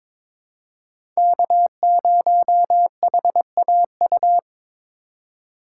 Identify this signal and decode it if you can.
Morse code, heard as a keyed tone, about 22 wpm, 700 Hz. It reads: K0HAU
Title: USA Call Signs - 5 Characters - 2x Character Spacing - 22wpm